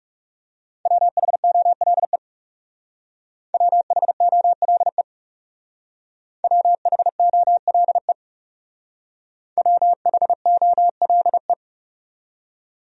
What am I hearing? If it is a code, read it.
Morse code, heard as a keyed tone, about 37 wpm, 700 Hz. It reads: WHOLE WHOLE WHOLE WHOLE
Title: Speed-Racing Top 500 Words 30wpm